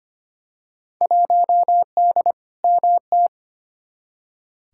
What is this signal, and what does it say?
Morse code, heard as a keyed tone, about 25 wpm, 700 Hz. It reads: 1D MT
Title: Field Day Exchanges - 25wpm